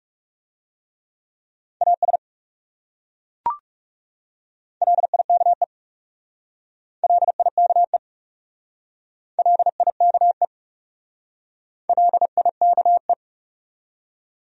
Morse code, heard as a keyed tone, about 36 wpm, 700 Hz. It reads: AS LIKE LIKE LIKE LIKE
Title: Speed-Racing Top 300 Words 30wpm